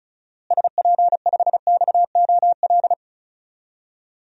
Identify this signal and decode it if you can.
Morse code, heard as a keyed tone, about 36 wpm, 700 Hz. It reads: SP5XOL
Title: DXCC-100 Call Signs - 6 Characters - Repeated 3x - 35wpm